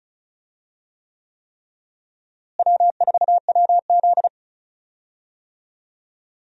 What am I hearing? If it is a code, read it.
Morse code, heard as a keyed tone, about 36 wpm, 700 Hz. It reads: W4WZ
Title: USA Call Signs - 4 Characters - Repeated 3x - 35wpm